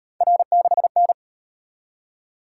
Morse code, heard as a keyed tone, about 37 wpm, 700 Hz. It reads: R6N